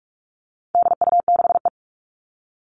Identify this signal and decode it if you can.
Morse code, heard as a keyed tone, about 44 wpm, 700 Hz. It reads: DU6E
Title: All Cut Numbers - 4-Digit 45wpm